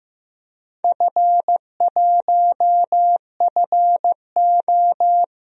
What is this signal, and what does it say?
Morse code, heard as a keyed tone, about 15 wpm, 700 Hz. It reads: F1FO
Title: DXCC-100 Call Signs - 4 Characters - 15wpm